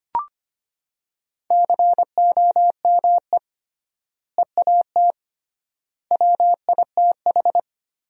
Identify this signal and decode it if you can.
Morse code, heard as a keyed tone, about 25 wpm, 700 Hz. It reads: COME EAT WITH
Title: 3x Word-Spacing - Sentences from Top 300 Words - 25wpm